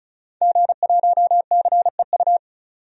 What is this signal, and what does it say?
Morse code, heard as a keyed tone, about 34 wpm, 700 Hz. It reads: G1CEU
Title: DXCC-100 Call Signs - 5 Characters - 35wpm